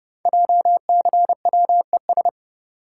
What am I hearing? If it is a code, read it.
Morse code, heard as a keyed tone, about 30 wpm, 700 Hz. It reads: JCWES
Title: Code Groups - 5 Letters - 30wpm